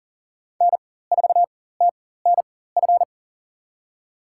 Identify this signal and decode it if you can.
Morse code, heard as a keyed tone, about 40 wpm, spaced as for 19 wpm, 700 Hz. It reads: N4TNF
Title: USA Call Signs - 5 Characters - 4x Character Spacing - 40wpm